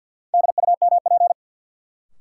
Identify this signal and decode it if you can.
Morse code, heard as a keyed tone, about 51 wpm, 700 Hz. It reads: DUMP
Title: Sentences from Top 1300 Words - New Word Drill Alt - 50wpm